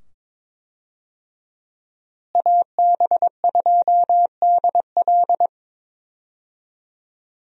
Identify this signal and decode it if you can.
Morse code, heard as a keyed tone, about 22 wpm, 700 Hz. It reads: AB2DL
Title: USA Call Signs - 5 Characters - Repeated 2x - 22wpm